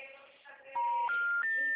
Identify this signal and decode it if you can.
Special information tone, with noise.